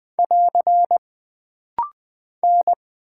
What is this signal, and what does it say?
Morse code, heard as a keyed tone, about 20 wpm, 700 Hz. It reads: <AR> N
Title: Single Character (Letter:Number:Punctuation:Prosign) - Rapid-Fire - 20wpm